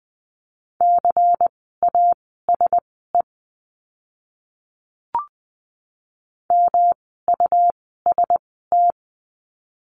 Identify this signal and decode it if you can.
Morse code, heard as a keyed tone, about 20 wpm, spaced as for 14 wpm, 700 Hz. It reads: CASE MUST
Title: 237 Lesson 38f - Letter q - Words 20wpm